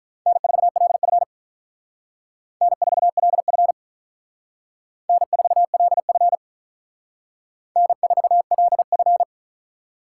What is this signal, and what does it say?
Morse code, heard as a keyed tone, about 46 wpm, 700 Hz. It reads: N4LF N4LF N4LF N4LF